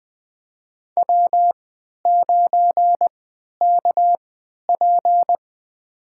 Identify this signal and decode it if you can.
Morse code, heard as a keyed tone, about 21 wpm, spaced as for 12 wpm, 700 Hz. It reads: W9KP
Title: USA Call Signs - 4 Characters - 3x Character Spacing - 20wpm